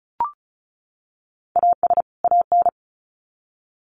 Morse code, heard as a keyed tone, about 34 wpm, 700 Hz. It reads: AS AN
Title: Sets of 2 Words 35wpm